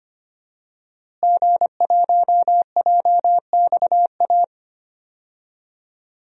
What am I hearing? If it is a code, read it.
Morse code, heard as a keyed tone, about 25 wpm, 700 Hz. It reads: G1JXA
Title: DXCC-100 Call Signs - 5 Characters - Repeated 2x - 25wpm